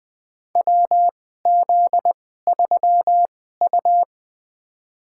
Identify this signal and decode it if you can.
Morse code, heard as a keyed tone, about 20 wpm, spaced as for 15 wpm, 700 Hz. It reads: WZ3U
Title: USA Call Signs - 2x Character Spacing - 20wpm